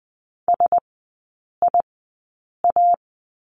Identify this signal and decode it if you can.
Morse code, heard as a keyed tone, about 20 wpm, spaced as for 8 wpm, 700 Hz. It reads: SIA
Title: 035 Lesson 7e - Letter s - Three Characters 20wpm